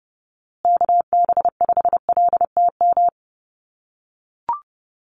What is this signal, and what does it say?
Morse code, heard as a keyed tone, about 30 wpm, 700 Hz. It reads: KB5LTM
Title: USA Call Signs Repeated 2x 30wpm